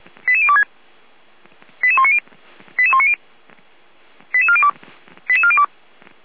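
Five-tone selective call (ZVEI), heard as quick runs of five tones, tones of about 80 ms. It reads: 88026 88190 88190 88402 88402
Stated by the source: Five-tone ZVEI selective call received on SDR, from feuerwehr-orlamuende.org (Feuerwehr Orlamünde, fire brigade).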